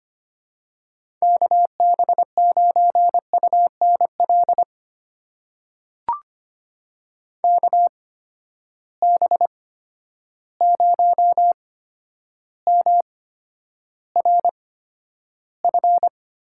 Morse code, heard as a keyed tone, about 25 wpm, spaced as for 11 wpm, 700 Hz. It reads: KB9UNL KB0MRF
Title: USA Call Signs - 6 Characters - 8x Character Spacing - 25wpm